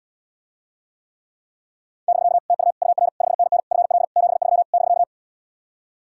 Morse code, heard as a keyed tone, about 79 wpm, 700 Hz. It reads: <BK> TU UR 5NN FL 73 <BK>